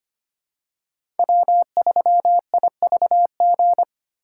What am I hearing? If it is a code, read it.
Morse code, heard as a keyed tone, about 25 wpm, 700 Hz. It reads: W3IVG
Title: USA Call Signs - 5 Characters - 4x Character Spacing - 25wpm